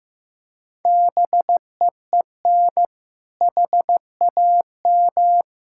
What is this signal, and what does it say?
Morse code, heard as a keyed tone, about 15 wpm, 700 Hz. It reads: BEEN HAM
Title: Practice QSO - Exchange 3 - More Info - 15wpm